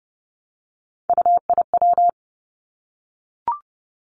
Morse code, heard as a keyed tone, about 30 wpm, 700 Hz. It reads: UIW
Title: Three Challenging Letter-Number Combinations 30wpm